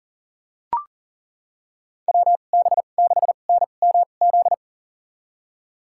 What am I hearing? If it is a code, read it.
Morse code, heard as a keyed tone, about 40 wpm, spaced as for 29 wpm, 700 Hz. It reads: WB6NMZ
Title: USA Call Signs - 6 Characters - 2x Character Spacing - 40wpm